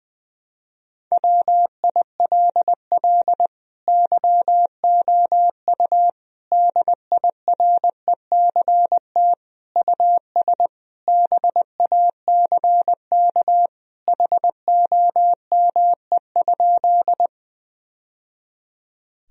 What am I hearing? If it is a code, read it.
Morse code, heard as a keyed tone, about 20 wpm, 700 Hz. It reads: WILL YOU DIRECT US BACK HOME?
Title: Sentences from Top 1000 Words - Review Part 2 - 20wpm